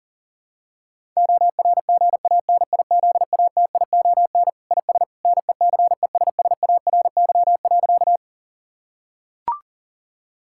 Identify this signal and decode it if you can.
Morse code, heard as a keyed tone, about 40 wpm, 700 Hz. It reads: ORGANIZATION IS NECESSARY.
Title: Sentences from Top 1000 Words - Review Part 5 - 40wpm